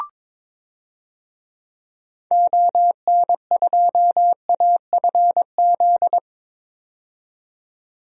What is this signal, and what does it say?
Morse code, heard as a keyed tone, about 21 wpm, 700 Hz. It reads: ON2AFZ
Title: DXCC-100 Call Signs - 6 Characters - 22wpm